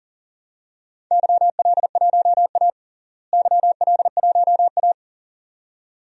Morse code, heard as a keyed tone, about 40 wpm, 700 Hz. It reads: YL1A YL1A
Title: DXCC-100 Call Signs - 4 Characters - Repeated 2x - 40wpm